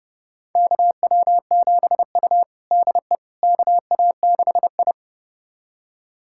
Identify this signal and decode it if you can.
Morse code, heard as a keyed tone, about 32 wpm, 700 Hz. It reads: KW7U DE KA6I
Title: USA Callsign DE Callsign - 4 Characters - 30wpm